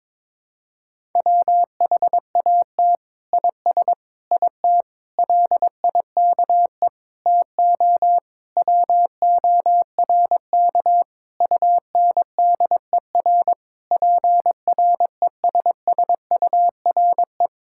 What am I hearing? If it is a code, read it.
Morse code, heard as a keyed tone, about 22 wpm, 700 Hz. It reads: WHAT IS IT LIKE TO WORK UNDER PRESSURE
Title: Sentences from Top 1000 Words - Review Part 6 - 22wpm